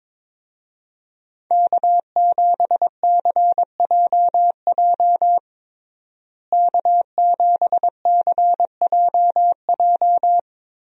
Morse code, heard as a keyed tone, about 22 wpm, 700 Hz. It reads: K7CJJ K7CJJ
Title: USA Call Signs - 5 Characters - Repeated 3x - 22wpm